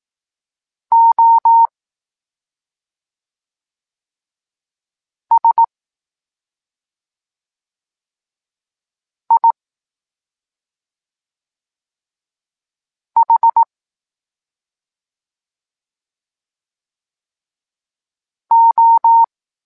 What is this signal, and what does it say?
Morse code, heard as a keyed tone, about 18 wpm, 920 Hz. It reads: O S I H O
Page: Tekst 14 – doskonalenie odbioru poznanych liter